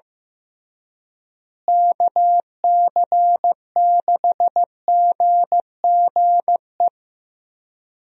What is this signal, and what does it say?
Morse code, heard as a keyed tone, about 15 wpm, 700 Hz. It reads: KC6GGE
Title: USA Call Signs - 6 Characters - Repeated 3x - 15wpm